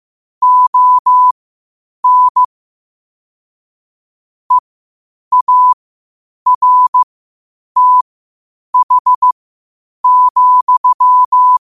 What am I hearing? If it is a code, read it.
Morse code, heard as a keyed tone, about 15 wpm, spaced as for 8 wpm, 1000 Hz. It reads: ON EARTH,
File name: 8wpm